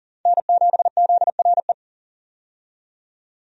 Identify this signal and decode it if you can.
Morse code, heard as a keyed tone, about 38 wpm, 700 Hz. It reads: N7ZRE
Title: USA Call Signs - 5 Characters - 40wpm